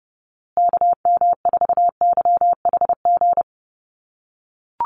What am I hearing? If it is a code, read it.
Morse code, heard as a keyed tone, about 30 wpm, 700 Hz. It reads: KM4YHG
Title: USA Call Signs Repeated 2x 30wpm